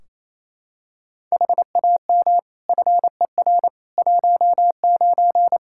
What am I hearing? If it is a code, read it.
Morse code, heard as a keyed tone, about 28 wpm, 700 Hz. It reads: HAM FER 19
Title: K1USN SST QSO Practice 28wpm